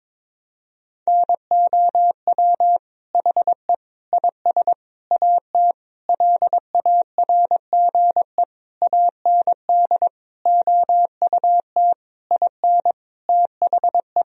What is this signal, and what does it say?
Morse code, heard as a keyed tone, about 22 wpm, 700 Hz. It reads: NOW HE IS AT LARGE AND OUT IN THE
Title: Sentences from Top 1000 Words - Review Part 1 - 22wpm